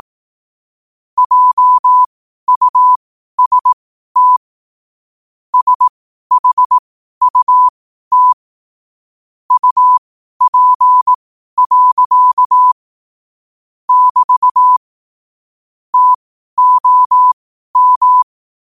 Morse code, heard as a keyed tone, about 18 wpm, spaced as for 12 wpm, 1000 Hz. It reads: JUST SHUT UP. <BT> TOM
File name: qotdfiles / 12wpm